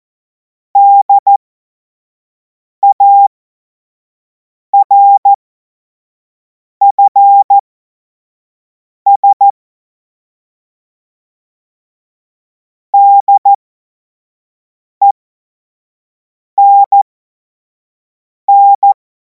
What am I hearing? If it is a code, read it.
Morse code, heard as a keyed tone, about 14 wpm, spaced as for 5 wpm, 800 Hz. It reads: DARFS DENN
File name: Morsekurs / Witze / Witz 24. März 2026 25 F70